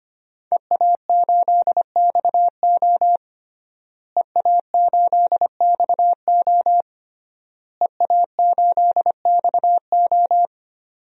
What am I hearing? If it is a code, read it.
Morse code, heard as a keyed tone, about 25 wpm, 700 Hz. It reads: EA8XO EA8XO EA8XO
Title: DXCC-100 Call Signs - 5 Characters - Repeated 3x - 25wpm